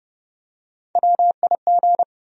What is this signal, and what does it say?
Morse code, heard as a keyed tone, about 30 wpm, 700 Hz. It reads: WIG
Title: USA Call Sign Postfixes - 3 Letters 30wpm